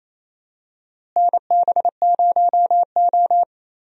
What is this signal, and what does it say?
Morse code, heard as a keyed tone, about 28 wpm, 700 Hz. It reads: NB0O
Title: USA Call Signs - 4 Characters - Repeated 2x - 28wpm